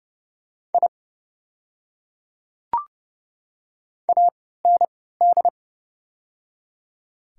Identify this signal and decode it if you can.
Morse code, heard as a keyed tone, about 30 wpm, spaced as for 17 wpm, 700 Hz. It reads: I AND
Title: 3x Character Spacing - Top 300 Words - 30wpm